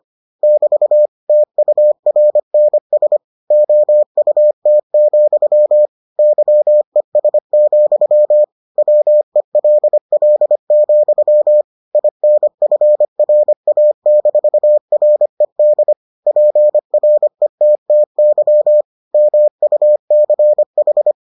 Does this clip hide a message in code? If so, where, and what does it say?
Yes: Morse code from 0.4 to 21.1 s, heard as a keyed tone, about 25 wpm, 600 Hz. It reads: <BT> TURNS OUT, YES, WELL, INFRA-RED PRETTY MUCH